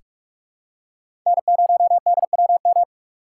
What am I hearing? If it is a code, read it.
Morse code, heard as a keyed tone, about 46 wpm, 700 Hz. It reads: N0DWM
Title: USA Call Signs - 5 Characters - 2x Character Spacing - 45wpm